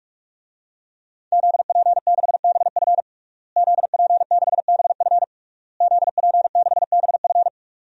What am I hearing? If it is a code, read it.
Morse code, heard as a keyed tone, about 48 wpm, 700 Hz. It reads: ZP6BF ZP6BF ZP6BF
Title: DXCC-100 Call Signs - 5 Characters - Repeated 3x - 45wpm